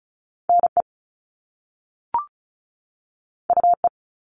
Morse code, heard as a keyed tone, about 36 wpm, 690 Hz. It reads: NE UE
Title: All Cut Numbers - 2-Digit 35wpm